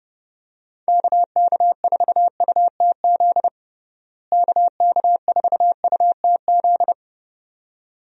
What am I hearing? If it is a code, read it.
Morse code, heard as a keyed tone, about 30 wpm, 700 Hz. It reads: KK4UTZ KK4UTZ
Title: USA Call Signs - 6 Characters - Repeated 2x - 30wpm